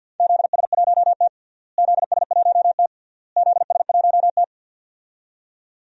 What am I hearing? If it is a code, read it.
Morse code, heard as a keyed tone, about 52 wpm, 700 Hz. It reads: ZS1T ZS1T ZS1T
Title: DXCC-100 Call Signs - 4 Characters - Repeated 3x - 50wpm